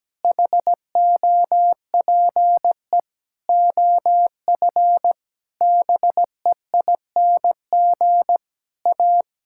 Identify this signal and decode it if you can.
Morse code, heard as a keyed tone, about 17 wpm, 700 Hz. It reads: HOPE OF BEING A
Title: Sentences from Top 1000 Words - Review Part 5 - 17wpm